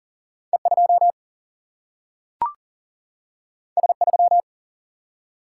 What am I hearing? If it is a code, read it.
Morse code, heard as a keyed tone, about 39 wpm, 700 Hz. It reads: E 2 S 3
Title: Single-Letter Single-Number 40wpm